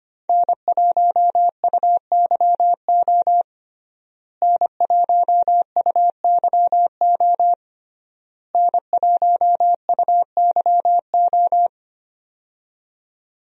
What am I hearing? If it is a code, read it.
Morse code, heard as a keyed tone, about 25 wpm, 700 Hz. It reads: N1UYO N1UYO N1UYO
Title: USA Call Signs - 5 Characters - Repeated 3x - 25wpm